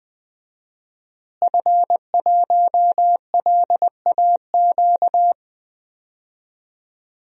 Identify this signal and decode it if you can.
Morse code, heard as a keyed tone, about 20 wpm, 700 Hz. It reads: F1LAQ
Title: DXCC-100 Call Signs - 5 Characters - Repeated 2x - 20wpm